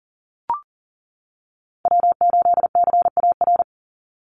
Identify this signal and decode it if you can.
Morse code, heard as a keyed tone, about 40 wpm, 700 Hz. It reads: W8CAR
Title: Call Signs - 40wpm